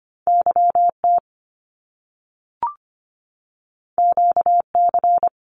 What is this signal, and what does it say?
Morse code, heard as a keyed tone, about 25 wpm, 700 Hz. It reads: YT QC
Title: Canadian Provinces and Territories 25wpm